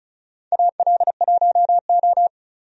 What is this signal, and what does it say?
Morse code, heard as a keyed tone, about 34 wpm, 700 Hz. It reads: AL1O
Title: USA Call Signs - 4 Characters - 35wpm